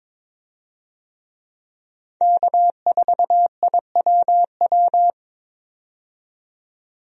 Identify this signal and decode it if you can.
Morse code, heard as a keyed tone, about 22 wpm, 700 Hz. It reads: K4IWW
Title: USA Call Signs - 5 Characters - 22wpm